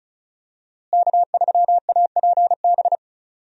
Morse code, heard as a keyed tone, about 36 wpm, 700 Hz. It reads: K3APB
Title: USA Call Signs - 5 Characters - Repeated 2x - 35wpm